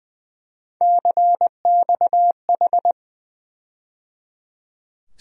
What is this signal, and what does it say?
Morse code, heard as a keyed tone, about 20 wpm, 700 Hz. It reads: CXH
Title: USA Call Sign Postfixes - 3 Letters 20wpm